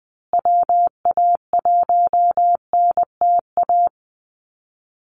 Morse code, heard as a keyed tone, about 20 wpm, 700 Hz. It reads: WA1NTA
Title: Call Signs Repeated 3x - 20wpm